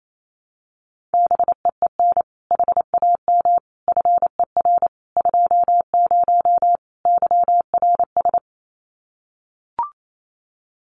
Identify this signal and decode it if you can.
Morse code, heard as a keyed tone, about 28 wpm, 700 Hz. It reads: BEEN HAM FER 20 YRS
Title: Common QSO Phrases 28wpm